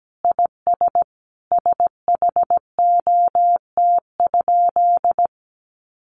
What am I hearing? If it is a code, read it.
Morse code, heard as a keyed tone, about 17 wpm, 700 Hz. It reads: IS SHOT?
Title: Sentences from Top 1100 Words 17wpm